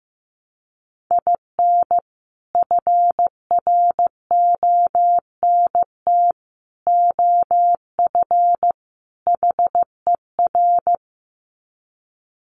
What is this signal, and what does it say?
Morse code, heard as a keyed tone, about 15 wpm, 700 Hz. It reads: IN FRONT OF HER
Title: Sets of 4 Words 15wpm